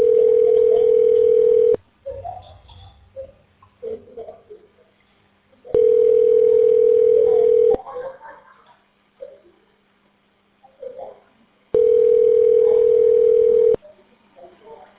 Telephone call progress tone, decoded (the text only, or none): ringback tone